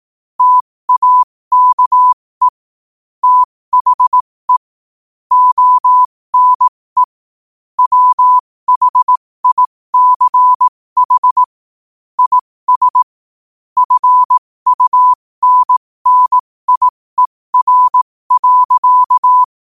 Morse code, heard as a keyed tone, about 18 wpm, spaced as for 15 wpm, 1000 Hz. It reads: TAKE THE ONE WHICH IS FUNNIER.
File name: qotdfiles / 15wpm